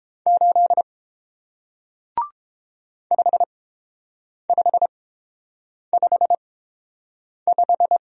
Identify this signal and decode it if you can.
Morse code, heard as a keyed tone, about 29 wpm, spaced as for 7 wpm, 700 Hz. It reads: 8 5555